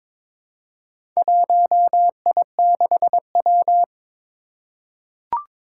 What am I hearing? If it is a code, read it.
Morse code, heard as a keyed tone, about 22 wpm, 700 Hz. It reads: 1I6W
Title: Four Challenging Letter-Number Combinations 22wpm